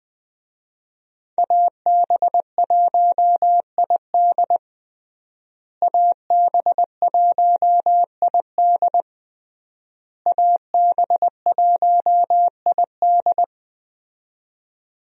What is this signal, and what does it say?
Morse code, heard as a keyed tone, about 20 wpm, 700 Hz. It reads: AB1ID AB1ID AB1ID
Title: USA Call Signs - 5 Characters - Repeated 3x - 20wpm